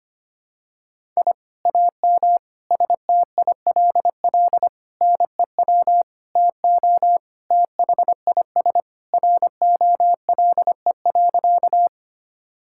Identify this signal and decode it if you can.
Morse code, heard as a keyed tone, about 25 wpm, 700 Hz. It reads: I AM STILL NEW TO THIS ROLE.
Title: Sentences from Top 1000 Words - Review Part 6 - 25wpm